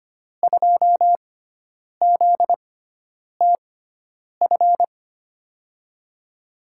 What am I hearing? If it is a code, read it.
Morse code, heard as a keyed tone, about 25 wpm, spaced as for 9 wpm, 700 Hz. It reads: 2ZTF
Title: USA Call Signs - 6 Characters - 6x Character Spacing - 25wpm